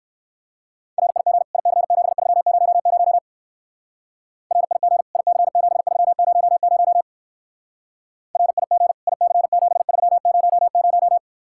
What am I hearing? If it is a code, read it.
Morse code, heard as a keyed tone, about 58 wpm, 700 Hz. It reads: RIG IC7300 RIG IC7300 RIG IC7300